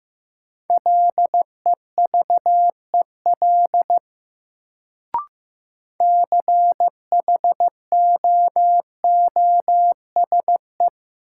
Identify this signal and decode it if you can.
Morse code, heard as a keyed tone, about 15 wpm, 700 Hz. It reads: LEVEL CHOOSE
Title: Sentences from Top 700 Words - New Word Drill - 15wpm